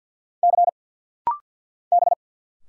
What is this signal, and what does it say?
Morse code, heard as a keyed tone, about 50 wpm, 700 Hz. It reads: C B
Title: Single Letters - Rapid-Fire 50wpm